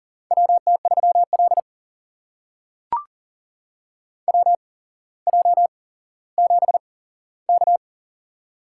Morse code, heard as a keyed tone, about 40 wpm, spaced as for 17 wpm, 700 Hz. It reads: WT3L WJ7K
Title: USA Call Signs - 4 Characters - 8x Character Spacing - 40wpm